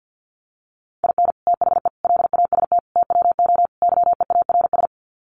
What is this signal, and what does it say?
Morse code, heard as a keyed tone, about 48 wpm, 700 Hz. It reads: IN THE LAST TWO YEARS
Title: Sets of 5 Words 50wpm